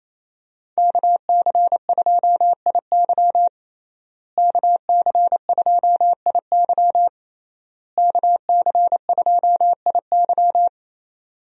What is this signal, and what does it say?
Morse code, heard as a keyed tone, about 28 wpm, 700 Hz. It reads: KC2IY KC2IY KC2IY
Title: USA Call Signs - 5 Characters - Repeated 3x - 28wpm